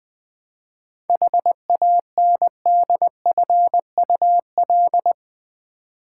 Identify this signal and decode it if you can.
Morse code, heard as a keyed tone, about 21 wpm, 700 Hz. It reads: HANDFUL
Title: 7 Letter Words - First 3 Letters Given - 20wpm